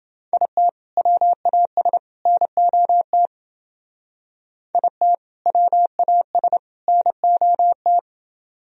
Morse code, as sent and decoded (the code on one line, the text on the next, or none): .. - / .-- .- ... / -. --- - / .. - / .-- .- ... / -. --- -
IT WAS NOT IT WAS NOT